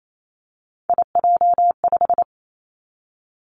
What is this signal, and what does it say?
Morse code, heard as a keyed tone, about 28 wpm, 700 Hz. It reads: IJ5
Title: Three Challenging Letter-Number Combinations 28wpm